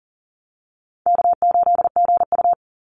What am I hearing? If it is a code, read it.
Morse code, heard as a keyed tone, about 40 wpm, 700 Hz. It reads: K8GU
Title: Call Signs - 40wpm